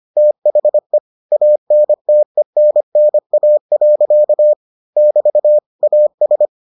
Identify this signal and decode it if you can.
Morse code, heard as a keyed tone, about 25 wpm, 600 Hz. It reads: THE ANTENNA. <BT> AS